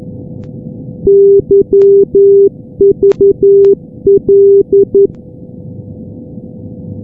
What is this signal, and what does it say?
Morse code, heard as a keyed tone, about 11 wpm, 400 Hz. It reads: YVL
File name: europe / g-443-yvl-343-sdr-hu-g-south west uk- -1245z